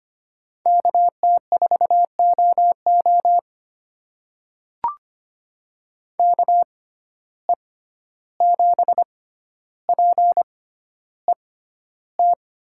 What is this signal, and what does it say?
Morse code, heard as a keyed tone, about 25 wpm, spaced as for 12 wpm, 700 Hz. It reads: KT4OO KE7PET
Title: USA Call Signs - 6x Character Spacing - 25wpm